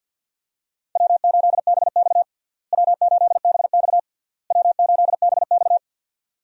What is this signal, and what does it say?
Morse code, heard as a keyed tone, about 50 wpm, 700 Hz. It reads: W8BX W8BX W8BX
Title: USA Call Signs - 4 Characters - Repeated 3x - 50wpm